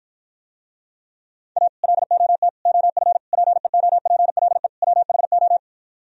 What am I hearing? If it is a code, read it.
Morse code, heard as a keyed tone, about 52 wpm, 700 Hz. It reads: A LOT OF PEOPLE WHO